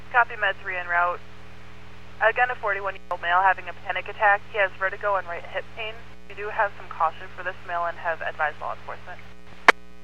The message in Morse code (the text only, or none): none